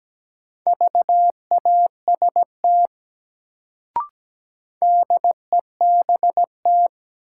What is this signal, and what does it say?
Morse code, heard as a keyed tone, about 17 wpm, 700 Hz. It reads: VAST DEBT
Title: Sentences from Top 1300 Words - New Word Drill Alt - 17wpm